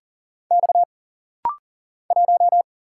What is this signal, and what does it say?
Morse code, heard as a keyed tone, about 40 wpm, 700 Hz. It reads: X 1
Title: Single Letter-Number Alt - Rapid-Fire 40wpm